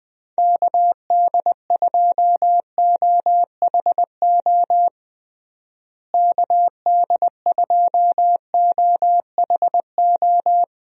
Morse code, heard as a keyed tone, about 20 wpm, 700 Hz. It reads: KD2OHO KD2OHO
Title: USA Call Signs - 6 Characters - Repeated 2x - 20wpm